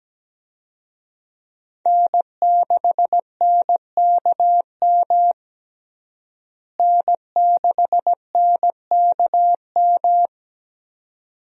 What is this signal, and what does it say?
Morse code, heard as a keyed tone, about 17 wpm, 700 Hz. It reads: N6NKM N6NKM
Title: USA Call Signs - 5 Characters - Repeated 2x - 17wpm